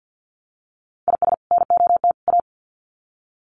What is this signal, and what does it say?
Morse code, heard as a keyed tone, about 50 wpm, 700 Hz. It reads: IS NOT A
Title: Sets of 3 Words 50wpm